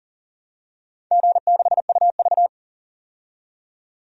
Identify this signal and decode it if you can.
Morse code, heard as a keyed tone, about 40 wpm, 690 Hz. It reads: G6UV
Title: DXCC-100 Call Signs - 4 Characters - 40wpm